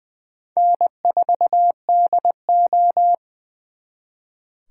Morse code, heard as a keyed tone, about 23 wpm, 700 Hz. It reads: N4DO
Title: Speed Racing - USA 1x2 Call Signs 20wpm